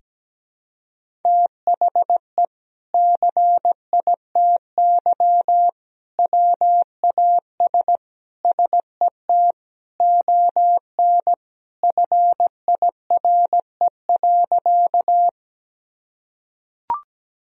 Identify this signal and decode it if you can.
Morse code, heard as a keyed tone, about 17 wpm, 700 Hz. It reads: THE CITY WAS SET ON FIRE.
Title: Sentences from Top 1000 Words - Review Part 1 - 17wpm